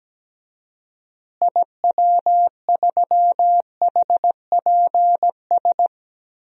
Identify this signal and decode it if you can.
Morse code, heard as a keyed tone, about 17 wpm, 700 Hz. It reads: IW3HPS
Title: DXCC-100 Call Signs - 6 Characters - Repeated 2x - 17wpm